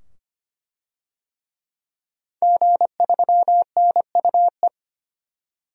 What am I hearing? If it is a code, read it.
Morse code, heard as a keyed tone, about 25 wpm, 700 Hz. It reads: G3NUE
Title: DXCC-100 Call Signs - 5 Characters - 25wpm